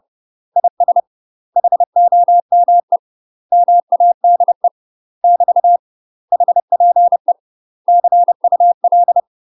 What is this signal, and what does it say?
Morse code, heard as a keyed tone, about 30 wpm, 700 Hz. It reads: IS HOME MADE <BT> HPE CUL
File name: UK MASTER QSOs TRAINING / 30WPM / UK MASTER QSOs 0047